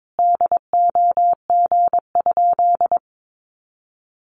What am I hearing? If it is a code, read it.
Morse code, heard as a keyed tone, about 22 wpm, 700 Hz. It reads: DOG?
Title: Sentences from Top 400 Words 22wpm